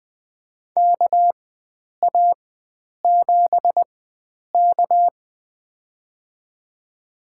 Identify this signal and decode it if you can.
Morse code, heard as a keyed tone, about 20 wpm, spaced as for 9 wpm, 700 Hz. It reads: KA7K
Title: USA Call Signs - 4 Characters - 4x Character Spacing - 20wpm